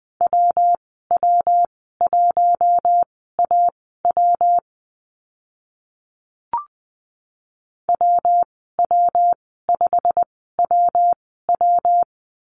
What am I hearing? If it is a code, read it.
Morse code, heard as a keyed tone, about 19 wpm, spaced as for 14 wpm, 700 Hz. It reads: WW1AW WW5WW
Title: 128 Lesson 21g - Number 6 - Call Signs 20wpm